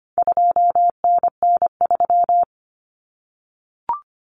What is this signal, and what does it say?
Morse code, heard as a keyed tone, about 25 wpm, 700 Hz. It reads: 2NN3
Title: Common Cut Numbers - 4-Digit 25wpm